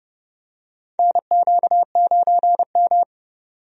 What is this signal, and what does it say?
Morse code, heard as a keyed tone, about 32 wpm, 700 Hz. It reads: NQ9M
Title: USA Call Signs - 4 Characters - 30wpm